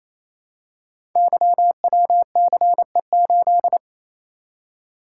Morse code, heard as a keyed tone, about 29 wpm, 700 Hz. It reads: YWCE8
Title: Code Groups - 5 Letter:Numbers - 28wpm